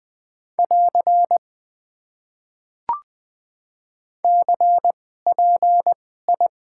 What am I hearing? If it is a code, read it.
Morse code, heard as a keyed tone, about 20 wpm, spaced as for 14 wpm, 700 Hz. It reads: <AR> CPI
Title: Top 90 QSO Elements - 2x Character Spacing - 20wpm